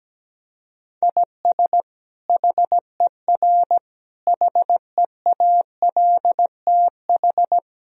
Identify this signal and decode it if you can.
Morse code, heard as a keyed tone, about 17 wpm, 700 Hz. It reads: IS HER HEALTH
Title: Sentences from Top 1000 Words - Review Part 3 - 17wpm